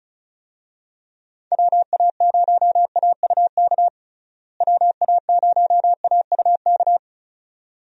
Morse code, heard as a keyed tone, about 34 wpm, 700 Hz. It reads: WA0AUK WA0AUK
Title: USA Call Signs - 6 Characters - Repeated 2x - 35wpm